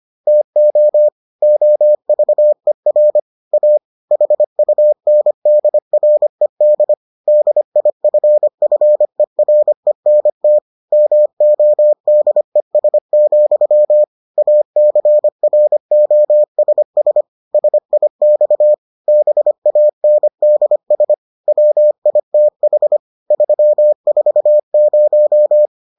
Morse code, heard as a keyed tone, about 25 wpm, 600 Hz. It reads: TO OVER A HUNDRED DIFFERENT MODES, ACROSS SIX BANDS WITH 340